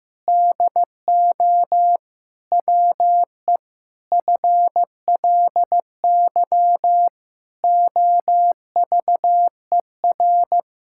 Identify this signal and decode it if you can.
Morse code, heard as a keyed tone, about 15 wpm, 700 Hz. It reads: DO WE FLY OVER
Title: Pangram Challenge v2 - 15wpm